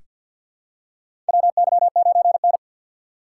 Morse code, heard as a keyed tone, about 47 wpm, 700 Hz. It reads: WY9N
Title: USA Call Signs - 4 Characters - 3x Character Spacing - 50wpm